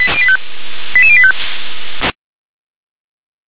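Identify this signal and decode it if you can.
Five-tone selective call (ZVEI), heard as quick runs of five tones, tones of about 70 ms. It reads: 88085
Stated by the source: ZVEI five-tone selective call 88747 received on SDR.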